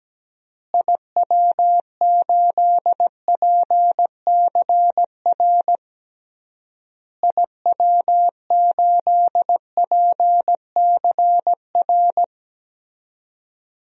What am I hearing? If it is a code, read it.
Morse code, heard as a keyed tone, about 17 wpm, 700 Hz. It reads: IW8PCR IW8PCR
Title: DXCC-100 Call Signs - 6 Characters - Repeated 3x - 17wpm